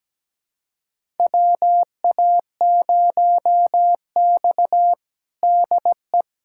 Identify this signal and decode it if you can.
Morse code, heard as a keyed tone, about 17 wpm, 700 Hz. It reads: WA0X DE
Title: USA Callsign DE Callsign - 4 Characters - 17wpm